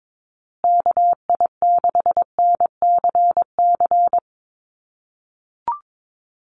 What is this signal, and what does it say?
Morse code, heard as a keyed tone, about 22 wpm, 700 Hz. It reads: KI6NCC
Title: USA Call Signs Repeated 3x 22wpm